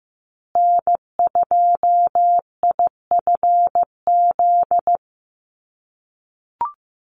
Morse code, heard as a keyed tone, about 15 wpm, 700 Hz. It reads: N2IFZ
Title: USA Call Signs Repeated 3x 15wpm